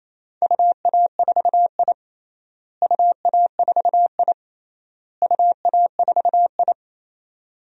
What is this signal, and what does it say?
Morse code, heard as a keyed tone, about 28 wpm, 700 Hz. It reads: UA4I UA4I UA4I
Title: DXCC-100 Call Signs - 4 Characters - Repeated 3x - 28wpm